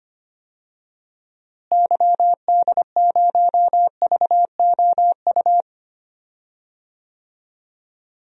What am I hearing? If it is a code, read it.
Morse code, heard as a keyed tone, about 25 wpm, 700 Hz. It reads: YD0VOU
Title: DXCC-100 Call Signs - 6 Characters - Repeated 3x - 25wpm